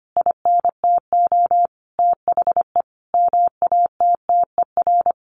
Morse code, heard as a keyed tone, about 25 wpm, 700 Hz. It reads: INTO THE MATTER
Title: Sentences from Top 1400 Words 25wpm